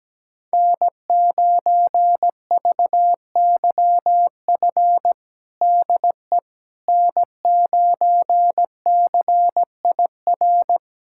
Morse code, heard as a keyed tone, about 17 wpm, 700 Hz. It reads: N9VYF DE N9CIR
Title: USA Callsign DE Callsign - 5 Characters - 17wpm